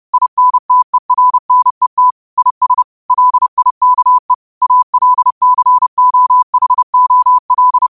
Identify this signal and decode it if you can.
Morse code, heard as a keyed tone, about 30 wpm, 1000 Hz. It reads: INTERNET IS LIKE ALCOHOL
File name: qotdfiles / 30wpm